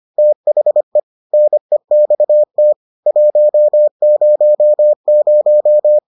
Morse code, heard as a keyed tone, about 25 wpm, 600 Hz. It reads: THE NEXT 100